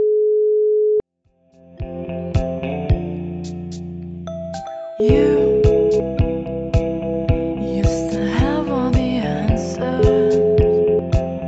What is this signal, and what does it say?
Ringback tone (Europe, 425 Hz), with noise.